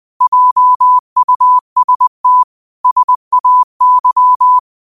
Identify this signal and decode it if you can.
Morse code, heard as a keyed tone, about 20 wpm, 1000 Hz. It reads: JUST SAY
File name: qotdfiles / 20wpm